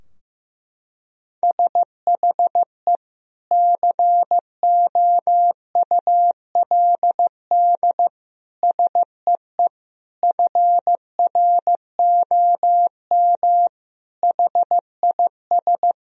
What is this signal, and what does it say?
Morse code, heard as a keyed tone, about 16 wpm, 700 Hz. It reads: SHE COULD SEE FROM HIS
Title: Sentences from Top 1000 Words - Review Part 2 - 15wpm